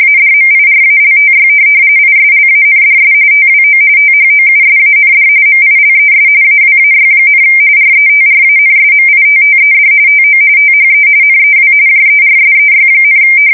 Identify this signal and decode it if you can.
RTTY, about 45.45 baud. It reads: ADMIRALTY ISLANDS TWO THOUSAND MILES AWAY, RETRANSMITTED TO THE NEW JERSEY,
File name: fsk170 / USS-New-Jersey